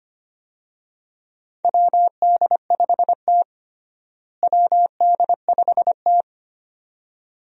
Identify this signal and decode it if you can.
Morse code, heard as a keyed tone, about 25 wpm, 700 Hz. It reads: WD5T WD5T
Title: USA Call Signs - 4 Characters - Repeated 2x - 25wpm